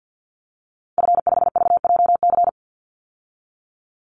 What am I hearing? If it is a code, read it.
Morse code, heard as a keyed tone, about 50 wpm, 700 Hz. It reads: F5VJC
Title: Call Signs Repeated 2x - 50wpm